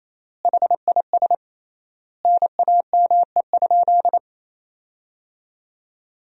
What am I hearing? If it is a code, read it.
Morse code, heard as a keyed tone, about 28 wpm, 700 Hz. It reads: HIS NAME?
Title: 3x Word-Spacing - Sentences from Top 400 Words - 28wpm